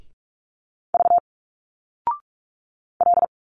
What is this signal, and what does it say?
Morse code, heard as a keyed tone, about 46 wpm, 700 Hz. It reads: V L
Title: Single Letters 45wpm